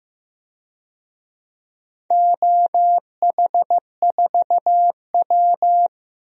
Morse code, heard as a keyed tone, about 15 wpm, 700 Hz. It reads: OH4W
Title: DXCC-100 Call Signs - 4 Characters - 15wpm